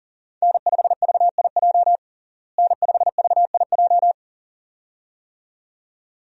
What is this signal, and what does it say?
Morse code, heard as a keyed tone, about 37 wpm, 700 Hz. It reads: N5VIJ N5VIJ
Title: USA Call Signs - 5 Characters - Repeated 2x - 40wpm